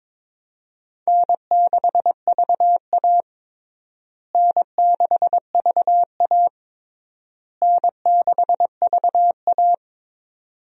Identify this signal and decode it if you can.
Morse code, heard as a keyed tone, about 22 wpm, 700 Hz. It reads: N6VA N6VA N6VA
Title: USA Call Signs - 4 Characters - Repeated 3x - 22wpm